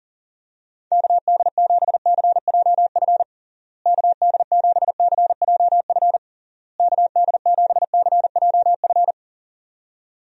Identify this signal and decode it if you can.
Morse code, heard as a keyed tone, about 37 wpm, 700 Hz. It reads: KD7CJF KD7CJF KD7CJF
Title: USA Call Signs - 6 Characters - Repeated 3x - 40wpm